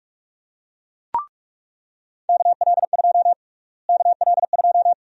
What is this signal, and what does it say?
Morse code, heard as a keyed tone, about 42 wpm, 700 Hz. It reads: KL2 KL2
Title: USA Call Sign Prefixes - Repeated 2x - 45wpm